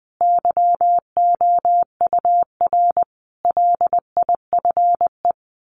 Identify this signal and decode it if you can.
Morse code, heard as a keyed tone, about 19 wpm, 700 Hz. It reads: YOUR LIFE
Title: Sets of 5 Words 20wpm